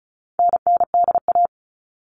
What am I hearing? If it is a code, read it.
Morse code, heard as a keyed tone, about 34 wpm, 700 Hz. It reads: NNDA
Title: All Cut Numbers - 4-Digit 35wpm